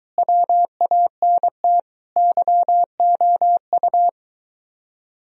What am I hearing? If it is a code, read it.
Morse code, heard as a keyed tone, about 24 wpm, 700 Hz. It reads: WANT YOU